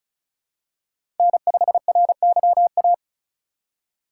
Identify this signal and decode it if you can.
Morse code, heard as a keyed tone, about 36 wpm, 690 Hz. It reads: N5RYA
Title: USA Call Signs - 3x Character Spacing - 35wpm